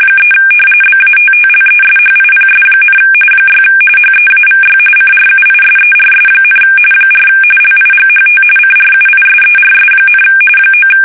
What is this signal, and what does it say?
RTTY, about 45.45 baud. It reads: 3 BATTLESHIPS, 6 HEAVY CRUISERS, 4 LIGHT CRUISERS, 9
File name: fsk850 / battleshipnj